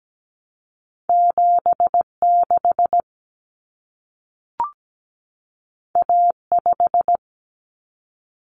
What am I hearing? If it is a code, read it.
Morse code, heard as a keyed tone, about 17 wpm, 700 Hz. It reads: 7 6 A 5
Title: Common Cut Numbers - 2-Digit 17wpm